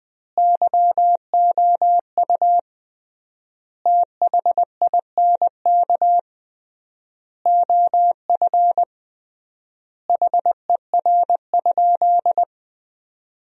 Morse code, heard as a keyed tone, about 20 wpm, 700 Hz. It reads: YOU THINK OF HER?
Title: 3x Word-Spacing - Sentences from Top 200 Words - 20wpm